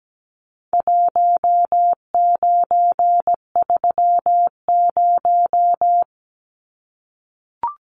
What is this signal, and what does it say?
Morse code, heard as a keyed tone, about 17 wpm, 700 Hz. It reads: 1930
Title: Numbers - Contest Style - 4-Digit 17wpm